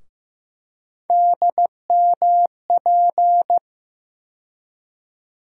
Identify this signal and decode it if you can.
Morse code, heard as a keyed tone, about 15 wpm, 700 Hz. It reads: DMP
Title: USA Call Sign Postfixes - 3 Letters 15wpm